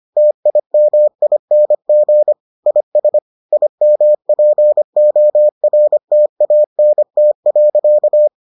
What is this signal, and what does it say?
Morse code, heard as a keyed tone, about 25 wpm, 600 Hz. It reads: TIMING IS IMPORTANT.